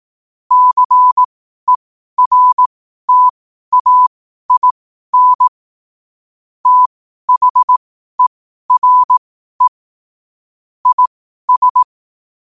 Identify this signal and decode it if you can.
Morse code, heard as a keyed tone, about 18 wpm, spaced as for 12 wpm, 1000 Hz. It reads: CERTAIN THERE IS
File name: qotdfiles / 12wpm